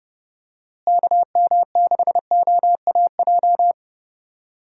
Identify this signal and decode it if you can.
Morse code, heard as a keyed tone, about 30 wpm, 700 Hz. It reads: KM6OAJ
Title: USA Call Signs - 6 Characters - Repeated 3x - 30wpm